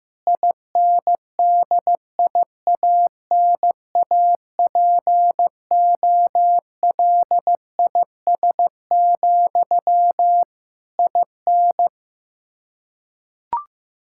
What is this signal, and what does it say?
Morse code, heard as a keyed tone, about 15 wpm, 700 Hz. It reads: INDIANAPOLIS, IN
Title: Practice QSO - Exchange 1 - Beginning - 15wpm